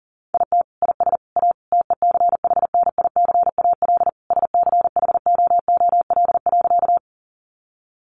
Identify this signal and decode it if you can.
Morse code, heard as a keyed tone, about 40 wpm, 700 Hz. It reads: IT IS A TECHNICAL SCHOOL.
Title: Sentences from Top 1100 Words 40wpm